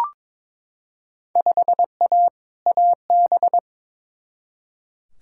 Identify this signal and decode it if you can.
Morse code, heard as a keyed tone, about 22 wpm, 700 Hz. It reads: 5A AB
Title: Field Day Exchanges Encore - 22wpm